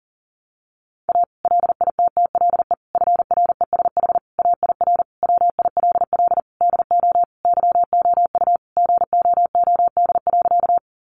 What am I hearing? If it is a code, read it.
Morse code, heard as a keyed tone, about 40 wpm, 700 Hz. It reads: A LITTLE FRESH AIR WILL DO YOU GOOD.
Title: Sentences from Top 1200 Words 40wpm